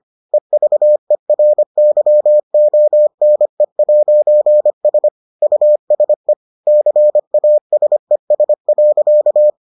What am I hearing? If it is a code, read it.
Morse code, heard as a keyed tone, about 25 wpm, 600 Hz. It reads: EVERYONE'S USE CASES.